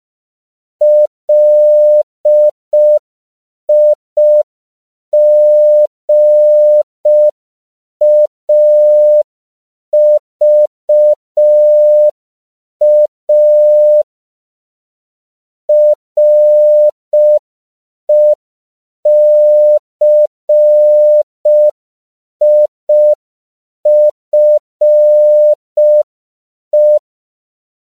Morse code, heard as a keyed tone, about 5 wpm, 600 Hz. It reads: LIGAVA RECIFE
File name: CW / 5PPM